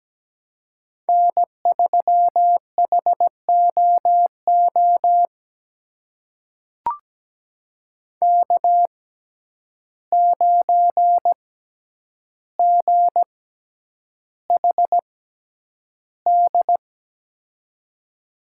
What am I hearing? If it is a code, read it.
Morse code, heard as a keyed tone, about 17 wpm, spaced as for 9 wpm, 700 Hz. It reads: N3HOO K9GHD
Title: USA Call Signs - 5 Characters - 6x Character Spacing - 17wpm